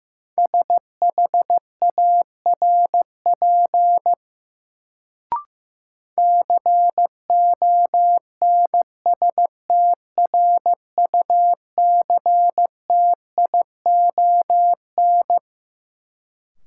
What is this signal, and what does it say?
Morse code, heard as a keyed tone, about 15 wpm, 700 Hz. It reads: SHARP CONSTRUCTION
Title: Sentences from Top 1600 Words - New Word Drill - 15wpm